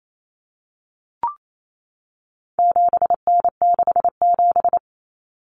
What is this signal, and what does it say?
Morse code, heard as a keyed tone, about 28 wpm, 700 Hz. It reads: 7N67
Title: Common Cut Numbers - 4-Digit 28wpm